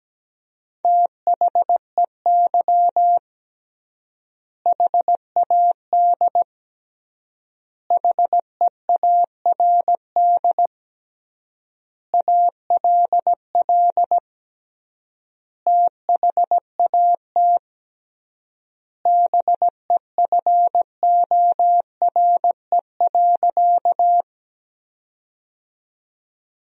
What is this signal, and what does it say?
Morse code, heard as a keyed tone, about 17 wpm, 700 Hz. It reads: THEY HAD HEARD ALL THAT BEFORE.
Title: 3x Word-Spacing - Sentences from Top 400 Words - 17wpm